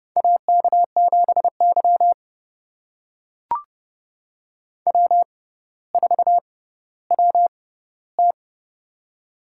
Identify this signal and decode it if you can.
Morse code, heard as a keyed tone, about 30 wpm, spaced as for 15 wpm, 700 Hz. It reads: AK7Y W4WT
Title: USA Call Signs - 4 Characters - 6x Character Spacing - 30wpm